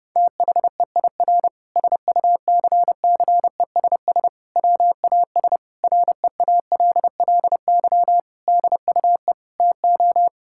Morse code, heard as a keyed tone, about 32 wpm, 700 Hz. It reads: THEIR SUCCESS WAS REALLY DUE TO
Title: Sentences from Top 1000 Words - Review Part 6 - 30wpm